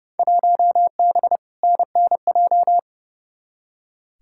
Morse code, heard as a keyed tone, about 30 wpm, 700 Hz. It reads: 1B NNJ
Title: Field Day Exchanges - 30wpm